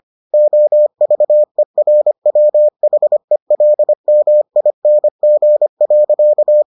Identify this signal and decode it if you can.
Morse code, heard as a keyed tone, about 25 wpm, 600 Hz. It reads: OVERWHELMING.